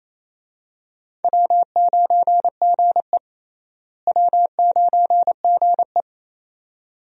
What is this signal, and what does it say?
Morse code, heard as a keyed tone, about 28 wpm, 700 Hz. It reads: W9GE W9GE
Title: USA Call Signs - 4 Characters - Repeated 2x - 28wpm